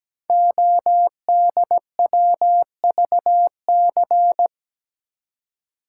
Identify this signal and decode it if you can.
Morse code, heard as a keyed tone, about 17 wpm, 700 Hz. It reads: ODWVC
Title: Code Groups - 5 Letters - 17wpm